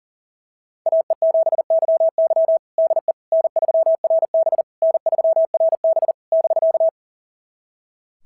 Morse code, heard as a keyed tone, about 40 wpm, 690 Hz. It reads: AE7YY DE N3RB N3RB <BK>
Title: SOTA QSO Practice 40wpm